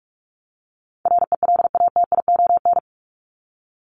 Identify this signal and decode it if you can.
Morse code, heard as a keyed tone, about 44 wpm, 700 Hz. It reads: RELATION
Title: Top 1751-2000 Words 45wpm